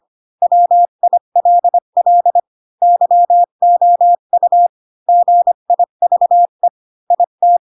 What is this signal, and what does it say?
Morse code, heard as a keyed tone, about 25 wpm, 700 Hz. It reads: WILL YOU GIVE IT